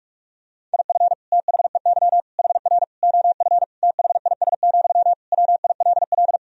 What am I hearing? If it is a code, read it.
Morse code, heard as a keyed tone, about 46 wpm, 700 Hz. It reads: IF THEY HR OF THIS, WILL
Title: Sentences from Top 400 Words and Abbreviations 45wpm